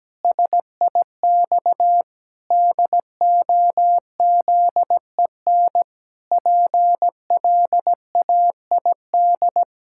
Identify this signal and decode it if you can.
Morse code, heard as a keyed tone, about 17 wpm, 700 Hz. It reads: SIX DOZEN PLAID
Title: Pangram Challenge v2 - 17wpm